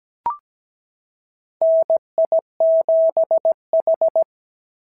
Morse code, heard as a keyed tone, about 16 wpm, 650 Hz. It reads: NI7H
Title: SOTA QSO Practice 17wpm